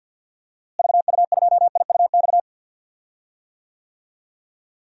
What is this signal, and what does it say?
Morse code, heard as a keyed tone, about 50 wpm, 700 Hz. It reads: VU2IUX
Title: DXCC-100 Call Signs - 6 Characters - Repeated 2x - 50wpm